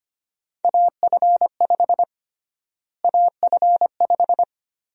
Morse code, heard as a keyed tone, about 25 wpm, 700 Hz. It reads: AF5 AF5
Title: USA Call Sign Prefixes - Repeated 2x - 25wpm